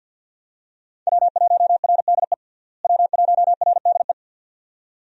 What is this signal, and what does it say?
Morse code, heard as a keyed tone, about 46 wpm, 700 Hz. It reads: W1RDE W1RDE
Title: USA Call Signs - 5 Characters - Repeated 2x - 50wpm